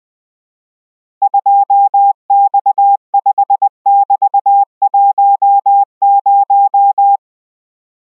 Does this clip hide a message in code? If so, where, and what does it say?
Yes: Morse code from 1.2 to 7.2 s, heard as a keyed tone, about 20 wpm, 800 Hz. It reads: 2X5<BT>10